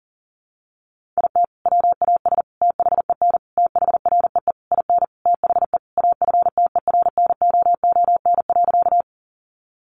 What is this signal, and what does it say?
Morse code, heard as a keyed tone, about 40 wpm, 700 Hz. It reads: IT WAS THEN THREE IN THE AFTERNOON.
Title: Sentences from Top 1500 Words 40wpm